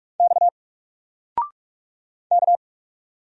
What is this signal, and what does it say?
Morse code, heard as a keyed tone, about 45 wpm, 700 Hz. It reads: X K
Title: Single Character (Letter:Number:Punctuation:Prosign) 45wpm